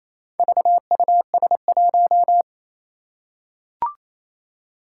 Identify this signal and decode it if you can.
Morse code, heard as a keyed tone, about 28 wpm, 700 Hz. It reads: VUS1
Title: Four Challenging Letter-Number Combinations 28wpm